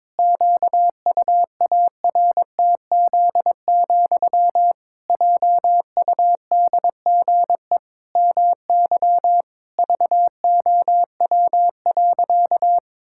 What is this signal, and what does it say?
Morse code, heard as a keyed tone, about 22 wpm, 700 Hz. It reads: QUARTZ, JUDGE MY VOW.
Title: Pangram Challenge v2 - 22wpm